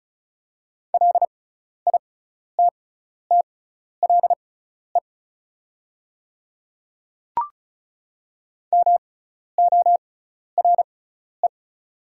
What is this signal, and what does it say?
Morse code, heard as a keyed tone, about 36 wpm, spaced as for 12 wpm, 690 Hz. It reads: LITTLE MORE
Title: 6x Character Spacing - Top 300 Words - 35wpm